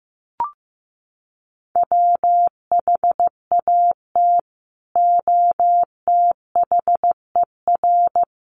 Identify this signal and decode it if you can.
Morse code, heard as a keyed tone, about 15 wpm, 700 Hz. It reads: WHAT OTHER
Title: Sentences from Top 100 Words 15wpm